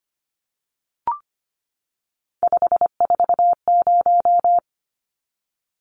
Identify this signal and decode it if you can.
Morse code, heard as a keyed tone, about 25 wpm, 700 Hz. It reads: 540
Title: Numbers - 3-Digit 25wpm